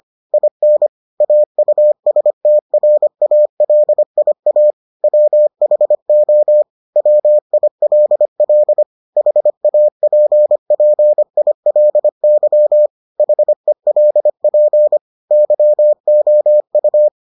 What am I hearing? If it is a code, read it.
Morse code, heard as a keyed tone, about 25 wpm, 600 Hz. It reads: IN AUSTRALIA WHO WILL HAPPILY HELP YOU